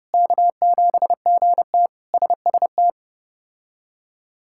Morse code, heard as a keyed tone, about 30 wpm, 700 Hz. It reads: K7GT SST
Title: K1USN SST QSO Practice - 30wpm